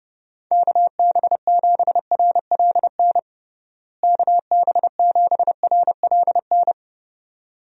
Morse code, heard as a keyed tone, about 30 wpm, 700 Hz. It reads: KB7RLN KB7RLN
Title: USA Call Signs - 6 Characters - Repeated 2x - 30wpm